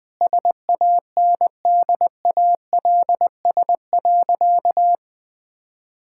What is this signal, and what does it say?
Morse code, heard as a keyed tone, about 20 wpm, 700 Hz. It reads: SANDALS.
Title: All-Age Appropriate Jokes - 20wpm